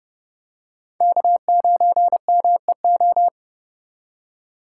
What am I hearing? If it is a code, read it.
Morse code, heard as a keyed tone, about 30 wpm, 700 Hz. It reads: K9MEO
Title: USA Call Signs - 5 Characters - 4x Character Spacing - 30wpm